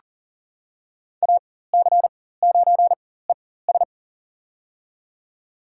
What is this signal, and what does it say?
Morse code, heard as a keyed tone, about 38 wpm, spaced as for 19 wpm, 690 Hz. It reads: AC9ES
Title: USA Call Signs - 4x Character Spacing - 40wpm